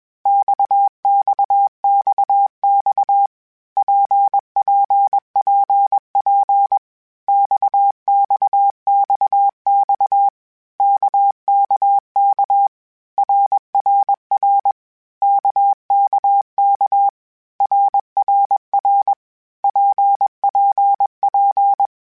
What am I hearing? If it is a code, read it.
Morse code, heard as a keyed tone, about 21 wpm, 800 Hz. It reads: XXXX PPPP XXXX KKK RRR KKK RRR PPP
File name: Groups@20 / Groups@20 mp3s / RKPX@20